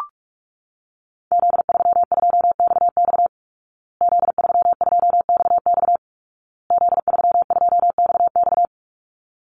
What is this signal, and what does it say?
Morse code, heard as a keyed tone, about 44 wpm, 700 Hz. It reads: Z32XX Z32XX Z32XX
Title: Call Signs Repeated 2x - 45wpm